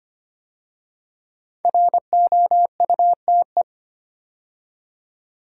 Morse code, heard as a keyed tone, about 25 wpm, 700 Hz. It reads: ROUTE
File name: 5 Letter Words - First 3 Letters Given - 25wpm